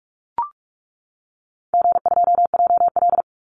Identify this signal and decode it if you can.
Morse code, heard as a keyed tone, about 44 wpm, 700 Hz. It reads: G2JL
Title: Call Signs - 45wpm